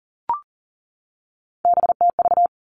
Morse code, heard as a keyed tone, about 40 wpm, 700 Hz. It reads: BTV
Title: Three Challenging Letter-Number Combinations 40wpm